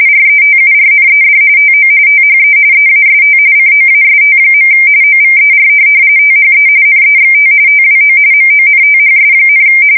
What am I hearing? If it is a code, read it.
RTTY, about 45.45 baud. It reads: RETURNED TO THE EAST COAST. OVER THE NEXT TWO MONTHS,
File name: fsk170 / USS-New-Jersey